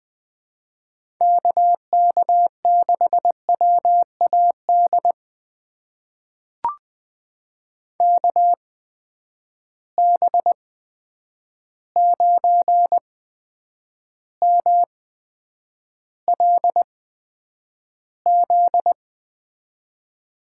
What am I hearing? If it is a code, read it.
Morse code, heard as a keyed tone, about 20 wpm, spaced as for 9 wpm, 700 Hz. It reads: KK6WAD KB9MLZ
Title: USA Call Signs - 8x Character Spacing - 20wpm